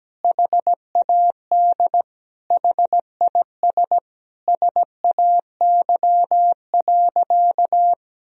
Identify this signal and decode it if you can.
Morse code, heard as a keyed tone, about 17 wpm, 700 Hz. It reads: HAD HIS SAY.
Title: Sentences from Top 1000 Words - Review Part 1 - 17wpm